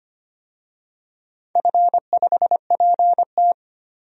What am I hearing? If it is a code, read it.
Morse code, heard as a keyed tone, about 25 wpm, 690 Hz. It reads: F5PT
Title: DXCC-100 Call Signs - 4 Characters - 25wpm